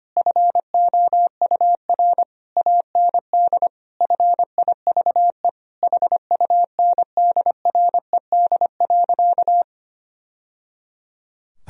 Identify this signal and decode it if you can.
Morse code, heard as a keyed tone, about 25 wpm, 700 Hz. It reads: FOUR AND FIVE HUNDRED.
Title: Sentences from Top 1000 Words - Review Part 2 - 25wpm